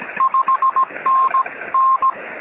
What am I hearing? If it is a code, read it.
Morse code, heard as a keyed tone, about 17 wpm, 1060 Hz, with noise. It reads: 5NN
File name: sm7lcb / 1911z dj8ms 23cm